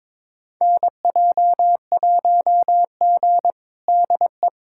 Morse code, heard as a keyed tone, about 22 wpm, 700 Hz. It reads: NJ1G DE
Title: USA Callsign DE Callsign - 4 Characters - 22wpm